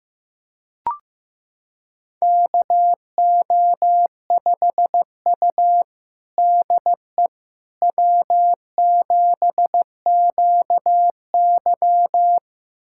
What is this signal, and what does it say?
Morse code, heard as a keyed tone, about 15 wpm, 700 Hz. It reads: KO5U DE W7QY
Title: USA Callsign DE Callsign - 4 Characters - 15wpm